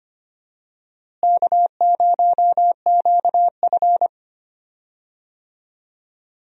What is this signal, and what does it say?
Morse code, heard as a keyed tone, about 25 wpm, 700 Hz. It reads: K0QF
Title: USA Call Signs - 4 Characters - Repeated 2x - 25wpm